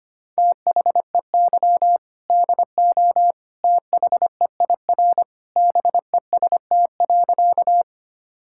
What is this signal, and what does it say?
Morse code, heard as a keyed tone, about 25 wpm, 700 Hz. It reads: THEY DO THEIR BEST.
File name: practice / 25wpm / Sentences with 2-Word Phrases - 25wpm